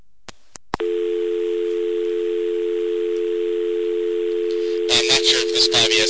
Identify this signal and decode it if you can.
Dial tone (North America), 350 + 440 Hz.